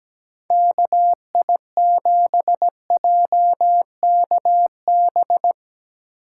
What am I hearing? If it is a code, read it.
Morse code, heard as a keyed tone, about 20 wpm, 700 Hz. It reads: KI7JKB
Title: Speed Racing - USA 2x3 Call Signs 17wpm